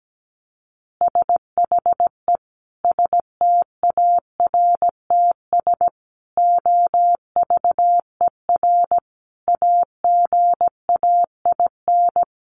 Morse code, heard as a keyed tone, about 17 wpm, 700 Hz. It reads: SHE STARTS OVER AGAIN
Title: Easy Phrases 17wpm